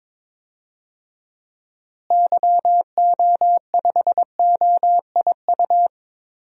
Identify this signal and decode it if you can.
Morse code, heard as a keyed tone, about 21 wpm, 700 Hz. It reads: YO5OIU
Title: DXCC-100 Call Signs - 6 Characters - Repeated 3x - 22wpm